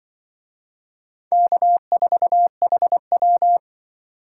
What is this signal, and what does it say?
Morse code, heard as a keyed tone, about 25 wpm, 700 Hz. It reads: K4HW
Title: Speed Racing - USA 1x2 Call Signs 20wpm